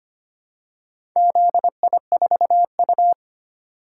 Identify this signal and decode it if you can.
Morse code, heard as a keyed tone, about 26 wpm, 690 Hz. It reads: ZI4U
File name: practice / 25wpm / Four Letter-Number Combinations - 25wpm